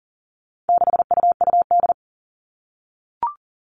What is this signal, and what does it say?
Morse code, heard as a keyed tone, about 40 wpm, 700 Hz. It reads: 6UUD
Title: All Cut Numbers - 4-Digit 40wpm